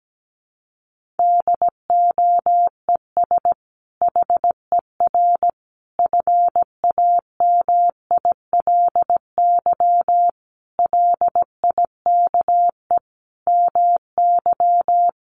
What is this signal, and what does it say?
Morse code, heard as a keyed tone, about 17 wpm, 700 Hz. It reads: DOES HER FAMILY LIKE MY
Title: Sentences from Top 1400 Words 17wpm